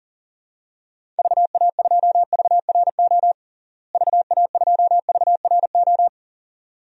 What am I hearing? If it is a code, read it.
Morse code, heard as a keyed tone, about 40 wpm, 700 Hz. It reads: VA2VRO VA2VRO
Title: DXCC-100 Call Signs - 6 Characters - Repeated 2x - 40wpm